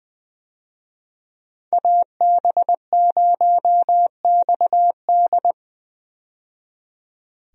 Morse code, heard as a keyed tone, about 20 wpm, 700 Hz. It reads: AB0XD
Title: USA Call Signs - 5 Characters - 20wpm